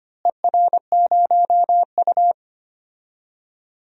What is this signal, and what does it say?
Morse code, heard as a keyed tone, about 25 wpm, 700 Hz. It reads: ER0U
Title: DXCC-100 Call Signs - 4 Characters - Repeated 2x - 25wpm